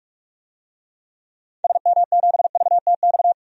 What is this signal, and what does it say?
Morse code, heard as a keyed tone, about 41 wpm, 700 Hz. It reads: SM7VTX
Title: DXCC-100 Call Signs - 6 Characters - 45wpm